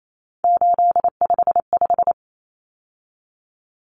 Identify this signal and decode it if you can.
Morse code, heard as a keyed tone, about 28 wpm, 700 Hz. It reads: 855
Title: Common Cut Numbers - 3-Digit 28wpm